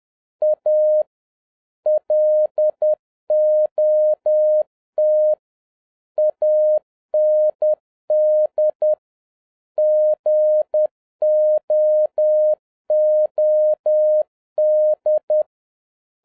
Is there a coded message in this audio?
Yes: Morse code from 0.4 to 15.4 s, heard as a keyed tone, about 10 wpm, 610 Hz. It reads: A LOT AND GOOD